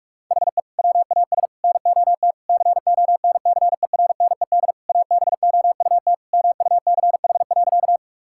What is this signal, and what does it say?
Morse code, heard as a keyed tone, about 46 wpm, 700 Hz. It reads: HE WAS NOT CONCERNED ABOUT MUCH.
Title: Sentences from Top 1000 Words - Review Part 6 - 45wpm